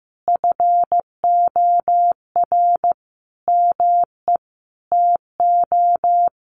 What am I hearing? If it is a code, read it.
Morse code, heard as a keyed tone, about 15 wpm, 700 Hz. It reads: FOR ME TO
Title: Sentences from Top 600 Words 15wpm